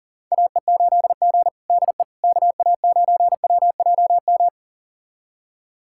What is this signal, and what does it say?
Morse code, heard as a keyed tone, about 40 wpm, 700 Hz. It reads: AE8G DE KA9WJM
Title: USA Callsign DE Callsign 40wpm